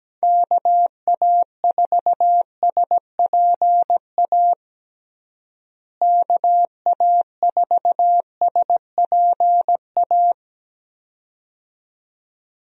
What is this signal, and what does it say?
Morse code, heard as a keyed tone, about 17 wpm, 700 Hz. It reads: KA4SPA KA4SPA
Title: USA Call Signs - 6 Characters - Repeated 2x - 17wpm